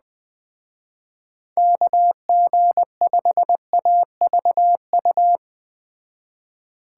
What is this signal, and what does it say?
Morse code, heard as a keyed tone, about 20 wpm, 700 Hz. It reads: KG5AVU